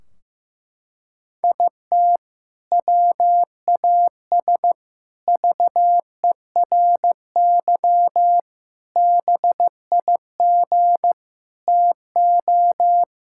Morse code, heard as a keyed tone, about 15 wpm, 700 Hz. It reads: IT WAS VERY BIG TO
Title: Sentences from Top 1600 Words 15wpm